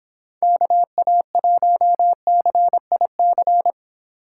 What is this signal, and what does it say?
Morse code, heard as a keyed tone, about 26 wpm, 700 Hz. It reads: KA1CIC